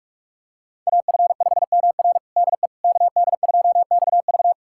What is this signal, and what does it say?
Morse code, heard as a keyed tone, about 47 wpm, 700 Hz. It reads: AF5MR DE KD2XV
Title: USA Callsign DE Callsign - 5 Characters - 45wpm